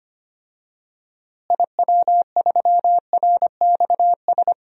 Morse code, heard as a keyed tone, about 25 wpm, 700 Hz. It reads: IW3RXS
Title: DXCC-100 Call Signs - 6 Characters - Repeated 3x - 25wpm